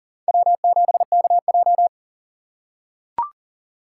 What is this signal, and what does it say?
Morse code, heard as a keyed tone, about 40 wpm, 700 Hz. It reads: W7KJ
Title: USA Call Signs - 3x Character Spacing - 40wpm